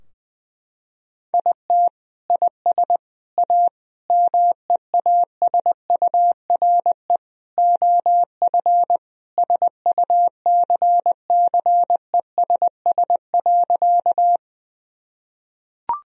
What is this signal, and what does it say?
Morse code, heard as a keyed tone, about 20 wpm, 700 Hz. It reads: IT IS A MEASURE OF SUCCESS.
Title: Sentences from Top 1000 Words - Review Part 2 - 20wpm